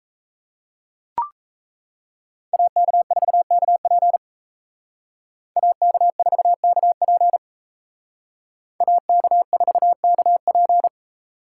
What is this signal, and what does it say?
Morse code, heard as a keyed tone, about 35 wpm, 700 Hz. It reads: AK4KP AK4KP AK4KP